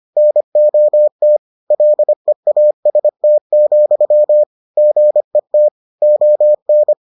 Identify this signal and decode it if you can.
Morse code, heard as a keyed tone, about 25 wpm, 600 Hz. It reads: NOT LEAST, GET ON